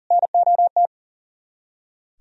Morse code, heard as a keyed tone, about 52 wpm, 700 Hz. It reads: NOT